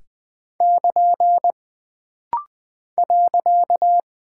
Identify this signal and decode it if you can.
Morse code, heard as a keyed tone, about 20 wpm, 700 Hz. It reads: <KN> .
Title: Single Punctuation-Prosign - Rapid-Fire - 20wpm